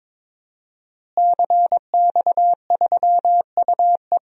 Morse code, heard as a keyed tone, about 22 wpm, 700 Hz. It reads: CX3UE
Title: DXCC-100 Call Signs - 5 Characters - Repeated 3x - 22wpm